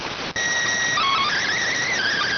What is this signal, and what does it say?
SSTV (slow-scan television), PD 290.